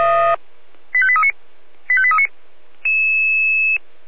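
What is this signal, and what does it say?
Five-tone selective call (ZVEI), heard as quick runs of five tones, tones of about 70 ms. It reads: 85839 85839
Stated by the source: ZVEI five-tone selective call 85830, received on SDR.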